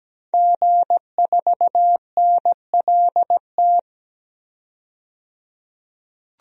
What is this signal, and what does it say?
Morse code, heard as a keyed tone, about 17 wpm, 700 Hz. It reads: G4NLT
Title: DXCC-100 Call Signs - 5 Characters - 17wpm